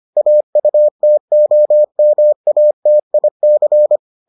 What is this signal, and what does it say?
Morse code, heard as a keyed tone, about 25 wpm, 600 Hz. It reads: AUTOMATIC